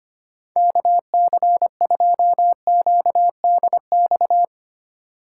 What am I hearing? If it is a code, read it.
Morse code, heard as a keyed tone, about 25 wpm, 700 Hz. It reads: KC2QDX
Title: USA Call Signs - 6 Characters - 3x Character Spacing - 25wpm